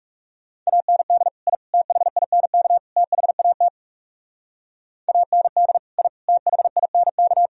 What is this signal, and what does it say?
Morse code, heard as a keyed tone, about 37 wpm, 700 Hz. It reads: AND I THINK THAT AND I THINK
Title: Speed-Racing - Sets of 4 Words 30wpm